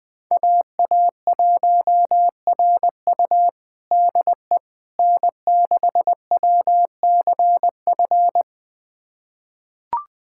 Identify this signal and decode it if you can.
Morse code, heard as a keyed tone, about 21 wpm, 700 Hz. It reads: AA1RU DE N6WCF
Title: USA Callsign DE Callsign - 5 Characters - 20wpm